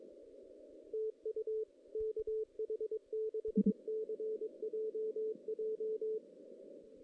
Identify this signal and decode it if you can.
Morse code, heard as a keyed tone, about 22 wpm, 440 Hz. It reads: TU KH6CJJ
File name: 2016 JAN CW NAQP / 2016 Jan naqp k5ot - 0-6